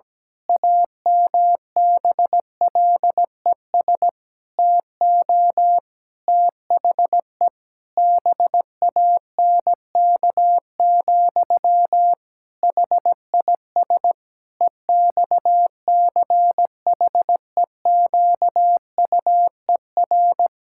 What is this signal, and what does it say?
Morse code, heard as a keyed tone, about 17 wpm, 700 Hz. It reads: AMBLES TO THE BANK, HIS EXCHEQUER
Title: Pangram Challenge v2 - 17wpm